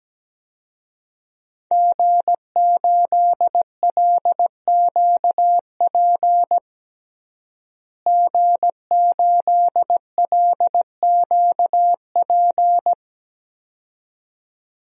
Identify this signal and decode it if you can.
Morse code, heard as a keyed tone, about 17 wpm, 700 Hz. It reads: G8LQP G8LQP
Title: DXCC-100 Call Signs - 5 Characters - Repeated 2x - 17wpm